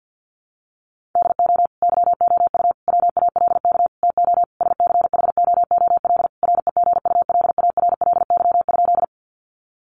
Morse code, heard as a keyed tone, about 50 wpm, 700 Hz. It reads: DO YOU WALK TO SCHOOL REGULARLY?
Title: Sentences from Top 1400 Words 50wpm